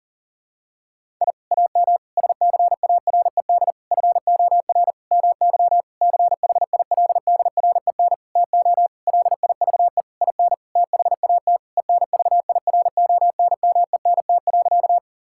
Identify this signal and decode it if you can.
Morse code, heard as a keyed tone, about 40 wpm, 700 Hz. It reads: I AM SCARED FOR MY CHILDREN TO LIVE IN THAT ENVIRONMENT.
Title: Sentences from Top 1000 Words - Review Part 5 - 40wpm